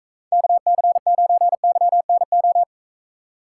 Morse code, heard as a keyed tone, about 44 wpm, 700 Hz. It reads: KC9YNO